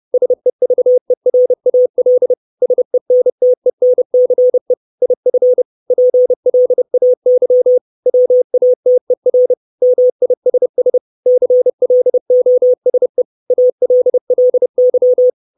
Morse code, heard as a keyed tone, about 30 wpm, 500 Hz. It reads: SEVERAL SENTENCE IF PLAY WATER MISS CLOSE ALLY
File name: 632 Words 30WPM